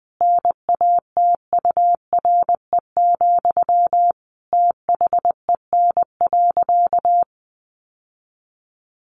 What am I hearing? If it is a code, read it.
Morse code, heard as a keyed tone, about 20 wpm, 700 Hz. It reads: NATURE, THEN.
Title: Sentences from Top 1200 Words 20wpm